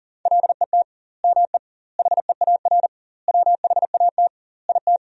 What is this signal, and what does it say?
Morse code, heard as a keyed tone, about 40 wpm, 700 Hz. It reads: LET ME HEAR WHAT IT
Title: 2x Word-Spacing - Sentences from Top 300 Words - 40wpm